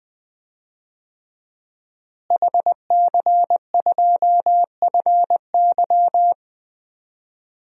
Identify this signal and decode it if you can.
Morse code, heard as a keyed tone, about 20 wpm, 700 Hz. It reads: HC2FY
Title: DXCC-100 Call Signs - 5 Characters - Repeated 2x - 20wpm